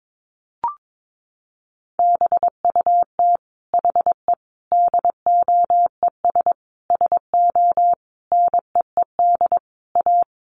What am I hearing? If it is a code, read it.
Morse code, heard as a keyed tone, about 22 wpm, 700 Hz. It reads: BUT HE DOES SO NEED A
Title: Sentences from Top 200 Words 22wpm